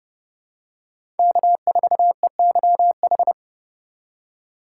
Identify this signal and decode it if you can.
Morse code, heard as a keyed tone, about 30 wpm, 700 Hz. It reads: K4EYH
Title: USA Call Signs - 5 Characters - 3x Character Spacing - 30wpm